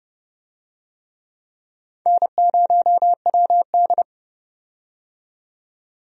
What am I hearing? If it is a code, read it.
Morse code, heard as a keyed tone, about 30 wpm, 700 Hz. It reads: N0WD
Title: USA Call Signs - 4 Characters - 30wpm